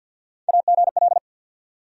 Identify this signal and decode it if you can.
Morse code, heard as a keyed tone, about 47 wpm, 700 Hz. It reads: AGL
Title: USA Call Sign Postfixes - 3 Letters 50wpm